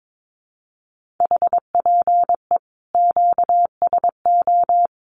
Morse code, heard as a keyed tone, about 22 wpm, 700 Hz. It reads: HPE QSO
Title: Common QSO Phrases 22wpm